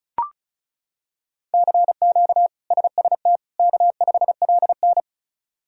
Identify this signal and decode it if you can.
Morse code, heard as a keyed tone, about 36 wpm, 700 Hz. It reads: CQ SST K5LN
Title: K1USN SST QSO Practice - 35wpm